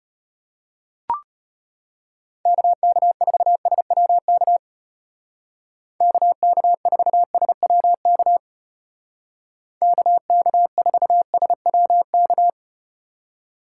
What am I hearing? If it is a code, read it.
Morse code, heard as a keyed tone, about 34 wpm, 700 Hz. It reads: KK4SWK KK4SWK KK4SWK